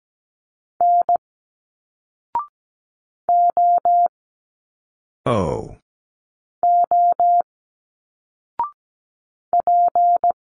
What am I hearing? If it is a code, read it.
Morse code, heard as a keyed tone, about 17 wpm, 700 Hz. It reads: N O O P
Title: Single Letters 17wpm